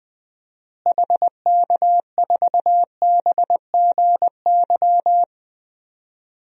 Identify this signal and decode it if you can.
Morse code, heard as a keyed tone, about 20 wpm, 700 Hz. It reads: HK4BGY
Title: DXCC-100 Call Signs - 6 Characters - Repeated 3x - 20wpm